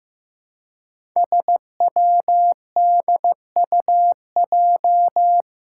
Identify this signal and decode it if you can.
Morse code, heard as a keyed tone, about 16 wpm, 700 Hz. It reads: SWDUJ
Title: Five Challenging Letter-Number Combinations 15wpm